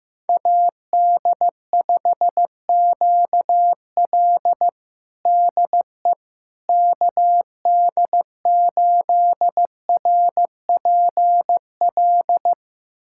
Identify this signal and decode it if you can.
Morse code, heard as a keyed tone, about 15 wpm, 700 Hz. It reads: AD5QL DE KD8RPL
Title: USA Callsign DE Callsign 15wpm